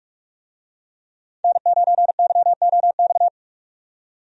Morse code, heard as a keyed tone, about 46 wpm, 700 Hz. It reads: N9YOX
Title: USA Call Signs - 5 Characters - 45wpm